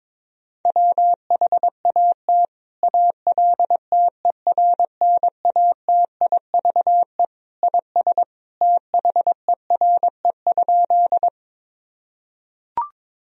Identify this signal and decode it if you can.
Morse code, heard as a keyed tone, about 22 wpm, 700 Hz. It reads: WHAT ALTERNATIVE IS THERE?
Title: Sentences from Top 1000 Words - Review Part 4 - 22wpm